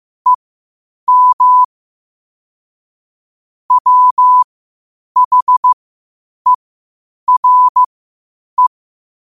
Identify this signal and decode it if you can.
Morse code, heard as a keyed tone, about 15 wpm, spaced as for 8 wpm, 1000 Hz. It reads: EM WHERE
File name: qotdfiles / 8wpm